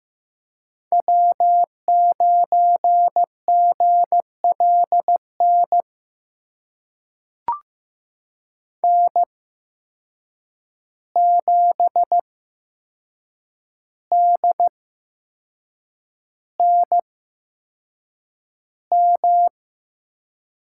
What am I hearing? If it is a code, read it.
Morse code, heard as a keyed tone, about 15 wpm, spaced as for 6 wpm, 700 Hz. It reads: W9GLN N7DNM
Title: USA Call Signs - 5 Characters - 8x Character Spacing - 15wpm